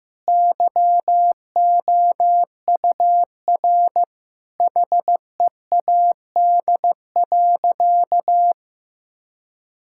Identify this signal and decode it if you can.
Morse code, heard as a keyed tone, about 15 wpm, 700 Hz. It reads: YOUR HEAD.
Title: Sentences from Top 1600 Words 15wpm